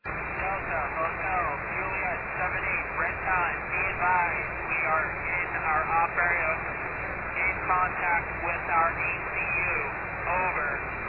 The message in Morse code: none

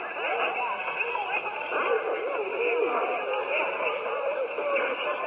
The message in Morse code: none

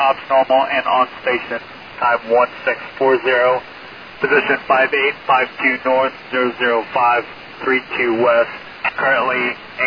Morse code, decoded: none

